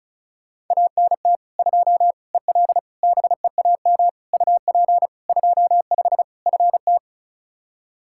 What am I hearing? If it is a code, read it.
Morse code, heard as a keyed tone, about 34 wpm, 700 Hz. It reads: ANT 2 EL BEAM UP 25 FT
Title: K1USN SST QSO Practice 35wpm